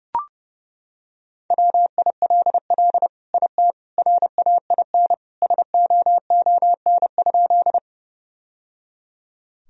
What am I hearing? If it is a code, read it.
Morse code, heard as a keyed tone, about 32 wpm, 700 Hz. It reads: WILL IT RAIN SOON?
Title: Sentences from Top 1000 Words - Review Part 1 - 30wpm